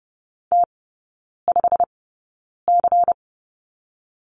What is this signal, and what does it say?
Morse code, heard as a keyed tone, about 30 wpm, spaced as for 9 wpm, 690 Hz. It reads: T5C
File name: Lesson 17e - Letter u - Three Characters 30wpm